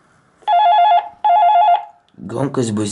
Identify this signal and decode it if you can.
There is no code.